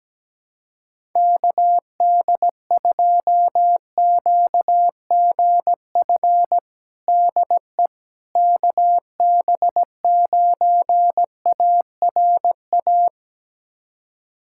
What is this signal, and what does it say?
Morse code, heard as a keyed tone, about 17 wpm, 700 Hz. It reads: KD2QGF DE KB9ARA
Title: USA Callsign DE Callsign - 6 Characters - 17wpm